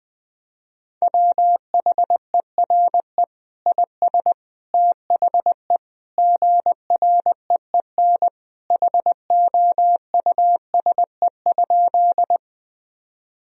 Morse code, heard as a keyed tone, about 21 wpm, 700 Hz. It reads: WHERE IS THE GREEN HOUSE?
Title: Sentences from Top 1000 Words - Review Part 2 - 20wpm